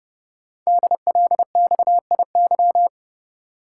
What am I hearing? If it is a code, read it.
Morse code, heard as a keyed tone, about 30 wpm, 700 Hz. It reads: DLXIY
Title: Code Groups - 5 Letters - 30wpm